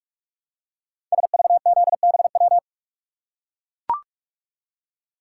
Morse code, heard as a keyed tone, about 44 wpm, 700 Hz. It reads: SVZBW
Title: Code Groups - 5 Letters - 45wpm